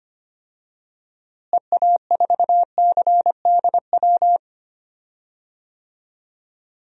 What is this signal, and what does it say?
Morse code, heard as a keyed tone, about 25 wpm, 700 Hz. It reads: EA4CDW
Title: DXCC-100 Call Signs - 6 Characters - Repeated 3x - 25wpm